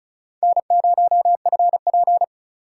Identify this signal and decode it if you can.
Morse code, heard as a keyed tone, about 36 wpm, 700 Hz. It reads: N0FP
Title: USA Call Signs - 4 Characters - 4x Character Spacing - 35wpm